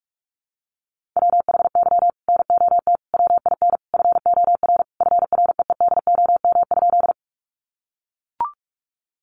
Morse code, heard as a keyed tone, about 44 wpm, 700 Hz. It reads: WHY NOT WIN FOR FREEDOM?
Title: Sentences from Top 1500 Words 45wpm